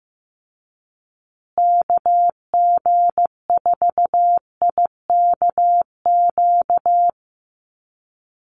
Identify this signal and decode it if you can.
Morse code, heard as a keyed tone, about 14 wpm, 700 Hz. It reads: KG4IKQ
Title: Call Signs Repeated 2x - 15wpm